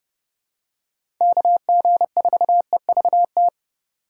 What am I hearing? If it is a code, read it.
Morse code, heard as a keyed tone, about 32 wpm, 700 Hz. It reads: KG4EVT
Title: USA Call Signs - 6 Characters - 4x Character Spacing - 30wpm